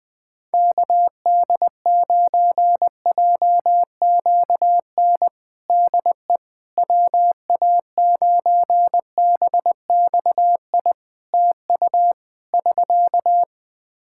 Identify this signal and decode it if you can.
Morse code, heard as a keyed tone, about 20 wpm, 700 Hz. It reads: KD9JQN DE WA9BXI TU <SK>
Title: Practice QSO - Exchange 4 - Ending - 20wpm